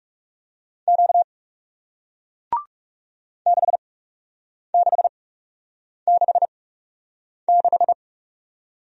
Morse code, heard as a keyed tone, about 41 wpm, spaced as for 8 wpm, 700 Hz. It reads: Q 6666